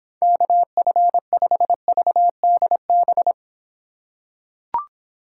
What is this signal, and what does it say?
Morse code, heard as a keyed tone, about 26 wpm, 700 Hz. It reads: KF5VDB